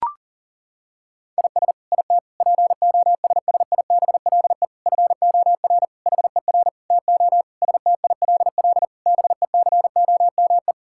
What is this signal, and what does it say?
Morse code, heard as a keyed tone, about 37 wpm, 700 Hz. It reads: IS IT POSSIBLE FOR HER TO STILL BECOME
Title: Sentences from Top 1000 Words - Review Part 6 - 40wpm